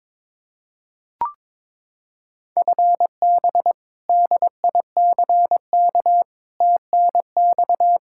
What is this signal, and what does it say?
Morse code, heard as a keyed tone, about 22 wpm, 700 Hz. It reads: FB DICK TNX
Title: Practice QSO - Exchange 3 - More Info - 22wpm